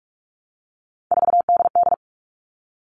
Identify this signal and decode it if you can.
Morse code, heard as a keyed tone, about 44 wpm, 700 Hz. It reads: 4DD
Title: All Cut Numbers - 3-Digit 45wpm